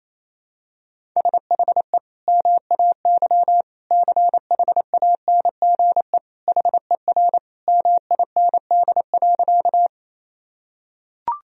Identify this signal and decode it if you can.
Morse code, heard as a keyed tone, about 28 wpm, 700 Hz. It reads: SHE MAY CHANGE HER MIND.
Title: Sentences from Top 1000 Words - Review Part 1 - 28wpm